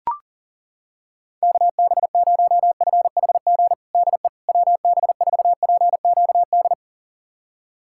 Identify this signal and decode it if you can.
Morse code, heard as a keyed tone, about 40 wpm, 700 Hz. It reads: KB0FHG DE WB4PQD
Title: USA Callsign DE Callsign - 6 Characters - 40wpm